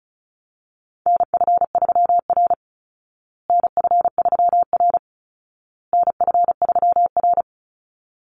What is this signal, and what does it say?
Morse code, heard as a keyed tone, about 34 wpm, 700 Hz. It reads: NF3R NF3R NF3R
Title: Call Signs Repeated 2x - 35wpm